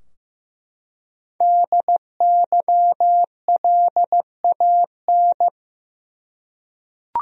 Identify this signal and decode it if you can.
Morse code, heard as a keyed tone, about 15 wpm, 700 Hz. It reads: DYLAN
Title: USA Common Names - Max 5 Letters - 15wpm